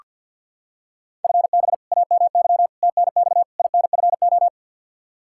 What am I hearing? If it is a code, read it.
Morse code, heard as a keyed tone, about 47 wpm, 700 Hz. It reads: FB AMY TNX INFO
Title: Practice QSO - Exchange 3 - More Info - 50wpm